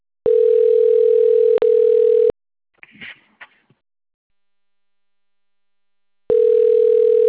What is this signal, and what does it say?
Ringback tone (North America), 440 + 480 Hz.
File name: rg-1600-unknown- - - .7082